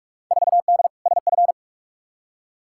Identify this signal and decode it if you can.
Morse code, heard as a keyed tone, about 46 wpm, 700 Hz. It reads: 4D SF
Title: Field Day Exchanges Encore - 45wpm